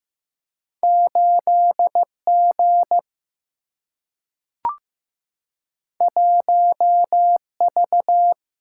Morse code, heard as a keyed tone, about 15 wpm, 700 Hz. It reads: 8G 1V
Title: Single-Letter Single-Number 15wpm